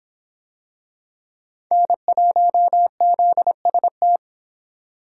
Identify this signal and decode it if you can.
Morse code, heard as a keyed tone, about 22 wpm, 700 Hz. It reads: N1ZST